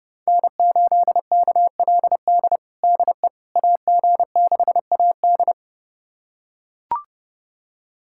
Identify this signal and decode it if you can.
Morse code, heard as a keyed tone, about 30 wpm, 700 Hz. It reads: N8KLD DE AG6AD
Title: USA Callsign DE Callsign - 5 Characters - 30wpm